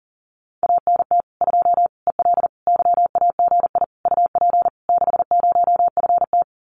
Morse code, heard as a keyed tone, about 40 wpm, 700 Hz. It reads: ANT 2 EL YAGI UP 60FT
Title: Common QSO Phrases Encore 40wpm